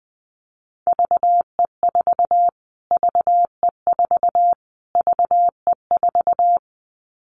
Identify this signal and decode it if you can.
Morse code, heard as a keyed tone, about 20 wpm, 700 Hz. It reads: VE4 VE4 VE4
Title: Callsign Prefixes for Top 25 DXCC - Repeated 3x - 20wpm